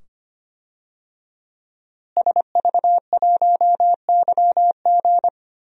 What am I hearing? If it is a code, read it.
Morse code, heard as a keyed tone, about 25 wpm, 700 Hz. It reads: SV1YG
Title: DXCC-100 Call Signs - 5 Characters - Repeated 2x - 25wpm